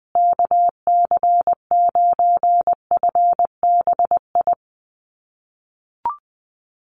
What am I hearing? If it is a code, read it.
Morse code, heard as a keyed tone, about 20 wpm, 700 Hz. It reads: KC9FBI
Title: USA Call Signs Repeated 2x 20wpm